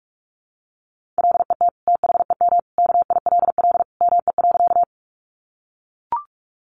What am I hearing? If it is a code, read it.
Morse code, heard as a keyed tone, about 44 wpm, 700 Hz. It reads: LET THEM KILL ME.
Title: Sentences from Top 1400 Words 45wpm